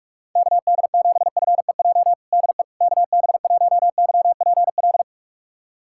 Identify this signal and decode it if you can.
Morse code, heard as a keyed tone, about 41 wpm, 700 Hz. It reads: KD7FEJ DE KB1YPL
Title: USA Callsign DE Callsign - 6 Characters - 45wpm